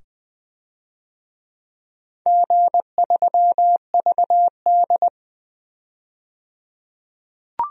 Morse code, heard as a keyed tone, about 20 wpm, 700 Hz. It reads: G3VD
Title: DXCC-100 Call Signs - 4 Characters - 20wpm